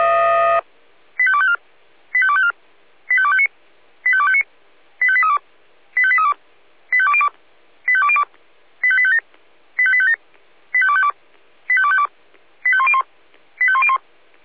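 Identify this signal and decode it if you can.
Five-tone selective call (ZVEI), heard as quick runs of five tones, tones of about 70 ms. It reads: 85254 85254 85259 85259 85832 85832 85202 85202 85857 85857 85252 85252 85101 85101
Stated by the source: ZVEI five-tone selective call received by SDR.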